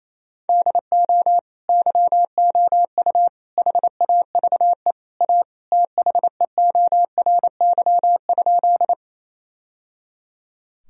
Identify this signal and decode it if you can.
Morse code, heard as a keyed tone, about 28 wpm, 700 Hz. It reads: DO YOU HAVE A THEORY?
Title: Sentences from Top 1000 Words - Review Part 3 - 28wpm